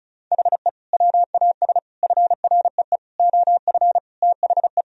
Morse code, heard as a keyed tone, about 36 wpm, 700 Hz. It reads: HE WAS FREE OF THE
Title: Sentences from Top 1000 Words - Review Part 4 - 35wpm